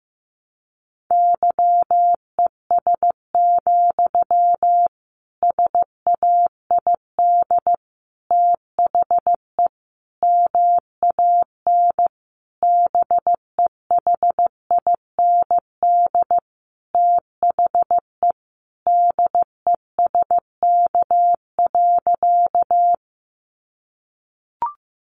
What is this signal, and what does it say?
Morse code, heard as a keyed tone, about 15 wpm, 700 Hz. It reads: YES, SAID THE MAN BEHIND THE DESK.
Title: A Sound Of Thunder 15wpm